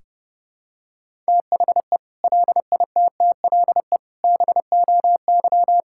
Morse code, heard as a keyed tone, about 32 wpm, 700 Hz. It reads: THE LITTLE BOY
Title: Sentences from Top 1600 Words 30wpm